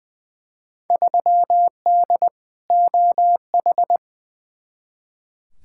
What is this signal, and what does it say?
Morse code, heard as a keyed tone, about 21 wpm, 700 Hz. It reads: 3D OH
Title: Field Day Exchanges - 20wpm